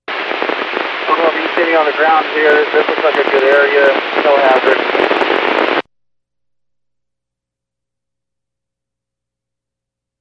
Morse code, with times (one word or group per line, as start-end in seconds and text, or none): none